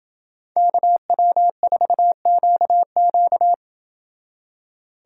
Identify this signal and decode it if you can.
Morse code, heard as a keyed tone, about 29 wpm, 700 Hz. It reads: KW4QQ